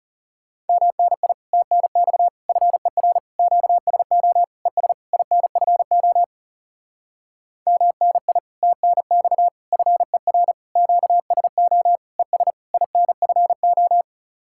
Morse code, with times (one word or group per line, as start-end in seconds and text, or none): 0.7-1.3 MNI
1.5-2.3 TNX
2.5-3.2 FER
3.4-4.4 QSO
4.7-4.9 ES
5.1-6.2 INFO
7.7-8.4 MNI
8.6-9.5 TNX
9.7-10.5 FER
10.8-12.0 QSO
12.2-12.5 ES
12.7-14.0 INFO